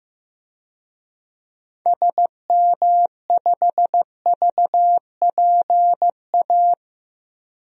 Morse code, heard as a keyed tone, about 16 wpm, 700 Hz. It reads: SM5VPA
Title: DXCC-100 Call Signs - 6 Characters - 15wpm